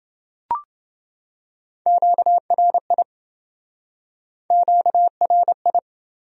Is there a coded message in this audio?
Yes: Morse code from 1.9 to 5.8 s, heard as a keyed tone, about 27 wpm, 700 Hz. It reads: QRI QRI